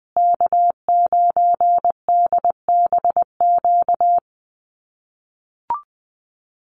Morse code, heard as a keyed tone, about 20 wpm, 700 Hz. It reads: K9DBQ
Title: USA Call Signs Repeated 3x 20wpm